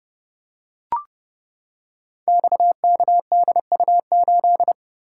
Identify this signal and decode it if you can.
Morse code, heard as a keyed tone, about 30 wpm, 700 Hz. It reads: XKDU8
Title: Code Groups - 5 Letter:Numbers - 30wpm